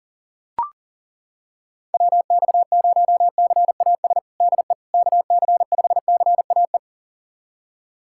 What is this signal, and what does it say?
Morse code, heard as a keyed tone, about 40 wpm, 700 Hz. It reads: WX0CAS DE KC5CAE
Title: USA Callsign DE Callsign - 6 Characters - 40wpm